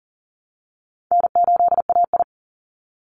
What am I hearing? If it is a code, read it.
Morse code, heard as a keyed tone, about 40 wpm, 700 Hz. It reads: N8AI
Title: Call Signs Repeated 3x - 40wpm